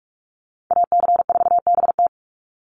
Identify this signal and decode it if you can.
Morse code, heard as a keyed tone, about 44 wpm, 700 Hz. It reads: AC4BT
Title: Call Signs Repeated 3x - 45wpm